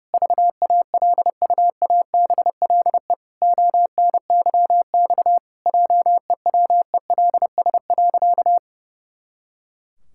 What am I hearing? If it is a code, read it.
Morse code, heard as a keyed tone, about 30 wpm, 700 Hz. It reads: VALUABLE ONYX JEWELS.
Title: Pangram Challenge v2 - 30wpm